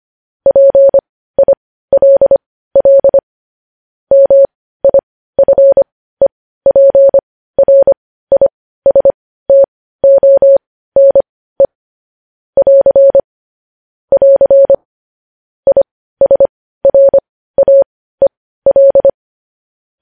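Morse code, heard as a keyed tone, about 25 wpm, spaced as for 15 wpm, 550 Hz. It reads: PILL MIFEPRISTONE <AR> <AR> ISRAEL
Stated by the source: Morse code news-headline practice sent at 15 words per minute.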